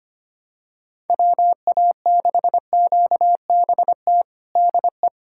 Morse code, heard as a keyed tone, about 25 wpm, 700 Hz. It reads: WA6QBT DE
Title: USA Callsign DE Callsign 25wpm